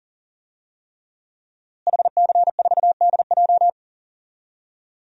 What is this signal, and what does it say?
Morse code, heard as a keyed tone, about 40 wpm, 700 Hz. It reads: HC4DJ
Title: DXCC-100 Call Signs - 5 Characters - Repeated 3x - 40wpm